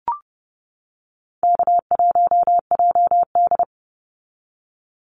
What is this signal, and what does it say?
Morse code, heard as a keyed tone, about 30 wpm, 700 Hz. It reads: K1JD
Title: NA SOTA Activator Callsigns 30wpm